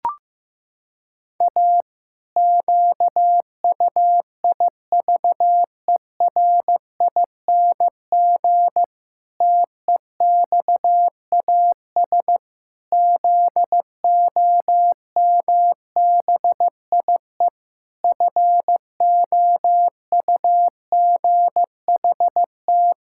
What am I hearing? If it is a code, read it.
Morse code, heard as a keyed tone, about 15 wpm, 700 Hz. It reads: A QUIVERING TEXAS ZOMBIE FOUGHT
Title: Pangram Challenge v2 - 15wpm